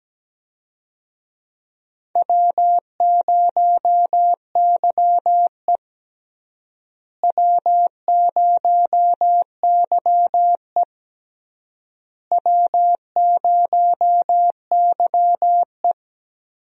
Morse code, heard as a keyed tone, about 17 wpm, 700 Hz. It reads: W0YE W0YE W0YE
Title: USA Call Signs - 4 Characters - Repeated 3x - 17wpm